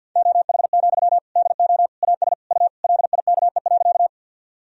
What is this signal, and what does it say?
Morse code, heard as a keyed tone, about 47 wpm, 700 Hz. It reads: OH, DO AS U LIKE.
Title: Sentences from Top 500 Words and Abbreviations 50wpm